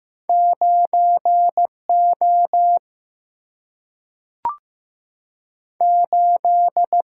Morse code, heard as a keyed tone, about 15 wpm, 700 Hz. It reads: 9O 8
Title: Single-Letter Single-Number 15wpm